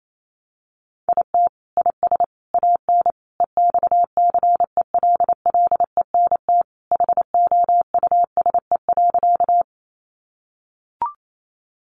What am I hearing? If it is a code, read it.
Morse code, heard as a keyed tone, about 28 wpm, 700 Hz. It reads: IT IS AN EXCELLENT HOUSE.
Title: Sentences from Top 1000 Words 28wpm